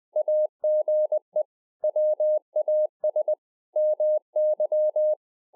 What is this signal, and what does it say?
Morse code, heard as a keyed tone, about 20 wpm, 610 Hz. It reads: AGE WAS MY
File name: 20WPM